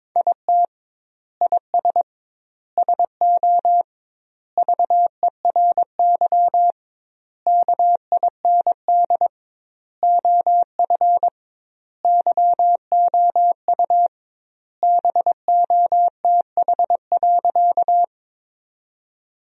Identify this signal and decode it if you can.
Morse code, heard as a keyed tone, about 22 wpm, 700 Hz. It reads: IT IS SO VERY KIND OF YOU BOTH.
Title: 2x Word-Spacing - Sentences from Top 300 Words - 22wpm